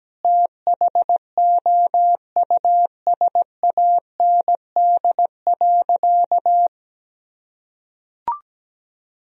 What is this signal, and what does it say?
Morse code, heard as a keyed tone, about 17 wpm, 700 Hz. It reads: THOUSAND.
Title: Sentences from Top 1000 Words - Review Part 3 - 17wpm